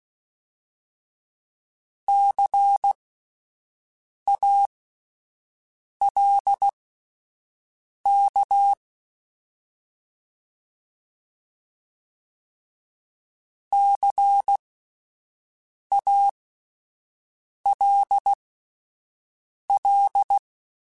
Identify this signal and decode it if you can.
Morse code, heard as a keyed tone, about 16 wpm, spaced as for 5 wpm, 800 Hz. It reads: CALK CALL
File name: Alphabetical / Alphabetical mp3s / Cs @5